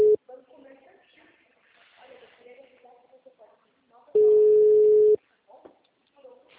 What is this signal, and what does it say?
Ringback tone (Europe, 425 Hz), with speech.